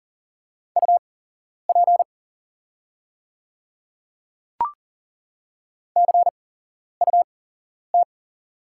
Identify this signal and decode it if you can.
Morse code, heard as a keyed tone, about 40 wpm, spaced as for 11 wpm, 700 Hz. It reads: UP CUT
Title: 8x Character Spacing - Top 500 Words - 40wpm